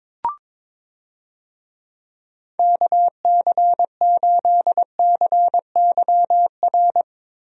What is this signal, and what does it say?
Morse code, heard as a keyed tone, about 22 wpm, 700 Hz. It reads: KC8CYR
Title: USA Call Signs - 6 Characters - Repeated 3x - 22wpm